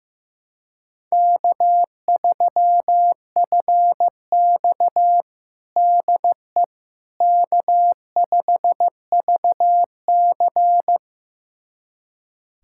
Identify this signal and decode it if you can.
Morse code, heard as a keyed tone, about 15 wpm, 700 Hz. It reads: K3FX DE K5VC
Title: USA Callsign DE Callsign - 4 Characters - 15wpm